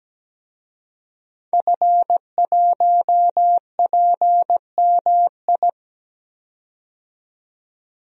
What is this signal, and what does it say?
Morse code, heard as a keyed tone, about 17 wpm, 700 Hz. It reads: F1PMI
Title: DXCC-100 Call Signs - 5 Characters - 17wpm